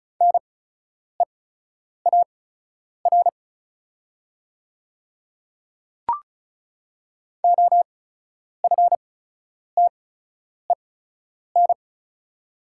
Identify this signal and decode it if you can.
Morse code, heard as a keyed tone, about 36 wpm, spaced as for 10 wpm, 690 Hz. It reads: NEAR OFTEN
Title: 8x Character Spacing - Top 300 Words - 35wpm